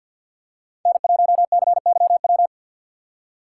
Morse code, heard as a keyed tone, about 47 wpm, 700 Hz. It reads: N1CYW
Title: USA Call Signs - 5 Characters - 4x Character Spacing - 50wpm